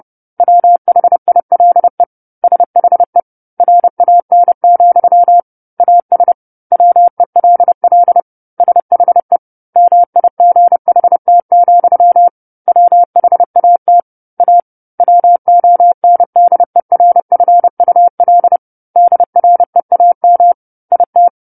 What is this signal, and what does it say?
Morse code, heard as a keyed tone, about 30 wpm, 700 Hz. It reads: WHILE SHE RAN, AS WELL SHE MIGHT, WHAT A WONDERFUL DREAM IT
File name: 10-Chapter-10 30 WPM